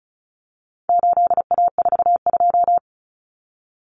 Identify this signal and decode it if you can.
Morse code, heard as a keyed tone, about 34 wpm, 700 Hz. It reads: 8A42
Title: Common Cut Numbers - 4-Digit 35wpm